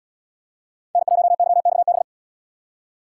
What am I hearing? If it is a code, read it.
Morse code, heard as a keyed tone, about 77 wpm, 700 Hz. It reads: N1YXB